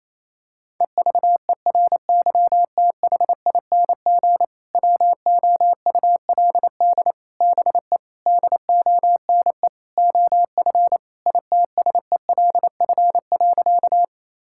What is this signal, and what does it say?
Morse code, heard as a keyed tone, about 28 wpm, 700 Hz. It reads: EVERYTHING WOULD BE DONE OF ITSELF.
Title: Sentences from Top 1000 Words - Review Part 5 - 28wpm